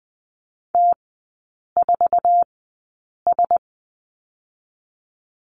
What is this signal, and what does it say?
Morse code, heard as a keyed tone, about 20 wpm, spaced as for 8 wpm, 700 Hz. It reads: T4S
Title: 047 Lesson 9e - Number 4 - Three Characters 20wpm